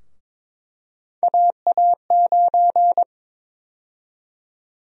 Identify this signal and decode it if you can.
Morse code, heard as a keyed tone, about 22 wpm, 700 Hz. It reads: AA9
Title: USA Call Sign Prefixes - 22wpm